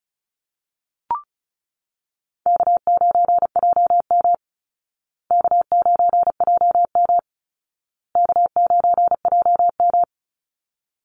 Morse code, heard as a keyed tone, about 34 wpm, 700 Hz. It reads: K9JM K9JM K9JM
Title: Call Signs Repeated 2x - 35wpm